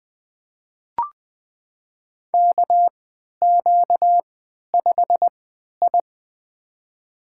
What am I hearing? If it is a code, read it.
Morse code, heard as a keyed tone, about 20 wpm, spaced as for 11 wpm, 700 Hz. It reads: KQ5I
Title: USA Call Signs - 3x Character Spacing - 20wpm